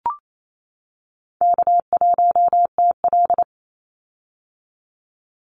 Morse code, heard as a keyed tone, about 28 wpm, 700 Hz. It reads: K1TL
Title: Call Signs - 28wpm